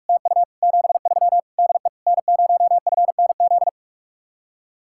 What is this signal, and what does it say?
Morse code, heard as a keyed tone, about 49 wpm, 700 Hz. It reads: TU 73 DE N0FNZ